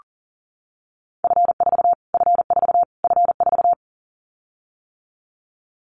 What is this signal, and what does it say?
Morse code, heard as a keyed tone, about 40 wpm, 700 Hz. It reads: F4 F4 F4
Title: Callsign Prefixes for Top 25 DXCC - Repeated 3x - 40wpm